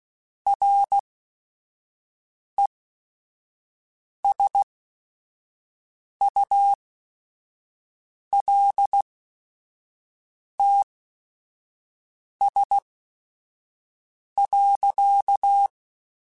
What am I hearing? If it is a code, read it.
Morse code, heard as a keyed tone, about 16 wpm, spaced as for 5 wpm, 800 Hz. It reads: RESULTS.